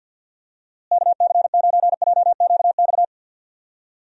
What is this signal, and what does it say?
Morse code, heard as a keyed tone, about 50 wpm, 700 Hz. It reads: KC9JQX
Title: USA Call Signs - 6 Characters - Repeated 3x - 50wpm